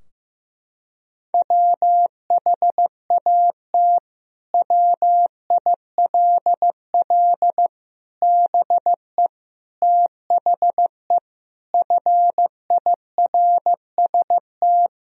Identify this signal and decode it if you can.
Morse code, heard as a keyed tone, about 15 wpm, 700 Hz. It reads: WHAT WILL BE THE FIRST
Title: Sentences from Top 1000 Words - Review Part 2 - 15wpm